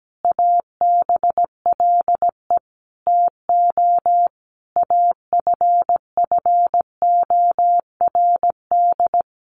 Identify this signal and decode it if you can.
Morse code, heard as a keyed tone, about 17 wpm, 700 Hz. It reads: ABLE TO AFFORD
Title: Sentences from Top 1100 Words 17wpm